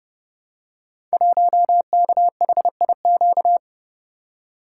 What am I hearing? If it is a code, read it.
Morse code, heard as a keyed tone, about 30 wpm, 700 Hz. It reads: 1KHIQ
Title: Code Groups - 5 Letter:Numbers - 30wpm